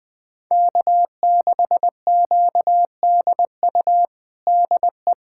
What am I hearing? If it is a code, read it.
Morse code, heard as a keyed tone, about 20 wpm, 700 Hz. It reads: K6QDU DE
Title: USA Callsign DE Callsign 20wpm